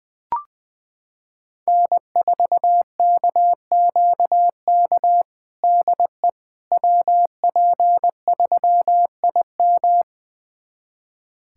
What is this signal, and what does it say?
Morse code, heard as a keyed tone, about 20 wpm, 700 Hz. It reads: N4KQK DE WP3IM
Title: USA Callsign DE Callsign - 5 Characters - 20wpm